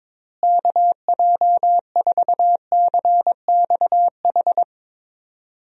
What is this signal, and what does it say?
Morse code, heard as a keyed tone, about 22 wpm, 700 Hz. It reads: KJ4CXH
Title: USA Call Signs - 6 Characters - Repeated 3x - 22wpm